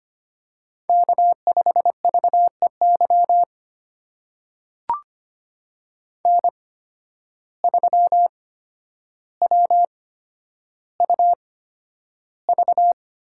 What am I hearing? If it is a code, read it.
Morse code, heard as a keyed tone, about 25 wpm, spaced as for 11 wpm, 700 Hz. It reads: K5VEY N3WUV
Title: USA Call Signs - 5 Characters - 8x Character Spacing - 25wpm